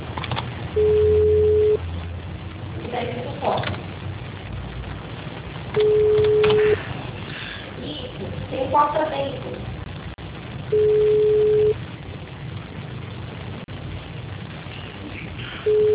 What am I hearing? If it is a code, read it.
Ringback tone (Europe, 425 Hz), with speech.
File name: out- -2007- - - .43671